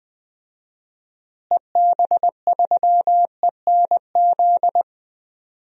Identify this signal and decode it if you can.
Morse code, heard as a keyed tone, about 20 wpm, 700 Hz. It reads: EB3ENZ
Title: DXCC-100 Call Signs - 6 Characters - Repeated 2x - 20wpm